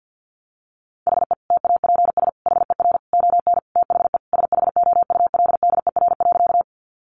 Morse code, heard as a keyed tone, about 50 wpm, 700 Hz. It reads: HE TAPS HER ON THE SHOULDER.
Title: Additional Easy Phrases 50wpm